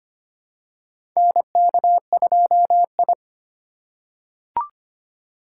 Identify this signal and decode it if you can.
Morse code, heard as a keyed tone, about 25 wpm, 700 Hz. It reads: NK2I
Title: USA Call Signs - 4 Characters - 3x Character Spacing - 25wpm